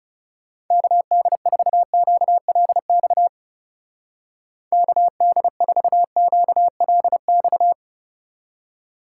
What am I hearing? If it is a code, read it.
Morse code, heard as a keyed tone, about 38 wpm, 700 Hz. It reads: KD4QLX KD4QLX